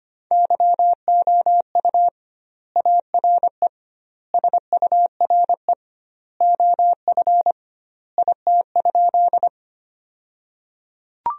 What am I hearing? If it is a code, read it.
Morse code, heard as a keyed tone, about 25 wpm, 700 Hz. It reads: YOU ARE SURE OF IT?
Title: 2x Word-Spacing - Sentences from Top 300 Words - 25wpm